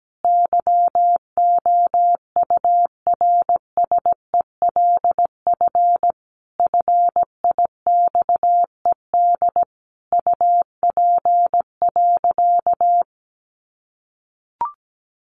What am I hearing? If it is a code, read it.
Morse code, heard as a keyed tone, about 17 wpm, 700 Hz. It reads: YOURSELF FIXED UP.
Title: Sentences from Top 1100 Words 17wpm